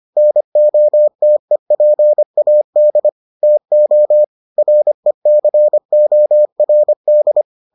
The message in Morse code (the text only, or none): NOTEPAD TO RECORD